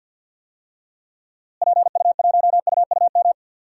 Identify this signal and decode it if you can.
Morse code, heard as a keyed tone, about 45 wpm, 700 Hz. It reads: PU1UUM
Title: DXCC-100 Call Signs - 6 Characters - Repeated 2x - 50wpm